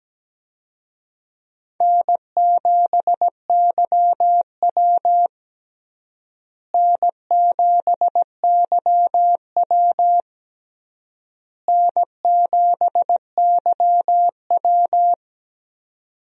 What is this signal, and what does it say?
Morse code, heard as a keyed tone, about 17 wpm, 700 Hz. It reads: N7YW N7YW N7YW
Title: USA Call Signs - 4 Characters - Repeated 3x - 17wpm